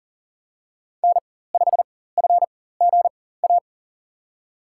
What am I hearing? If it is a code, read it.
Morse code, heard as a keyed tone, about 39 wpm, spaced as for 19 wpm, 700 Hz. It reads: N5FGA
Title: USA Call Signs - 4x Character Spacing - 40wpm